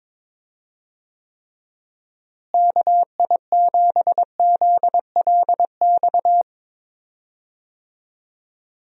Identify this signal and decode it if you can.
Morse code, heard as a keyed tone, about 22 wpm, 700 Hz. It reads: KI7ZLX
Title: USA Call Signs - 6 Characters - Repeated 2x - 22wpm